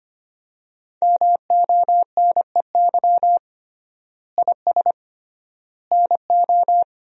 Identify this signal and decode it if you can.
Morse code, heard as a keyed tone, about 25 wpm, 700 Hz. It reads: MONEY IS NO
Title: 3x Word-Spacing - Sentences from Top 500 Words - 25wpm